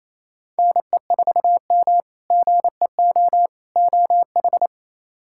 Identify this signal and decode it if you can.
Morse code, heard as a keyed tone, about 28 wpm, 700 Hz. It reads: NE4M GEO OH
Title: K1USN SST QSO Practice - 28wpm